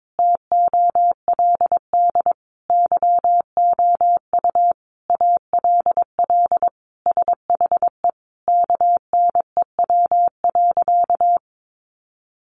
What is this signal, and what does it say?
Morse code, heard as a keyed tone, about 22 wpm, 700 Hz. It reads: TOLD YOU ALL SHE KNEW.
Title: Sentences from Top 1300 Words 22wpm